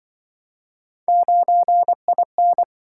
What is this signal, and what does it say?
Morse code, heard as a keyed tone, about 27 wpm, 700 Hz. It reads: 9IN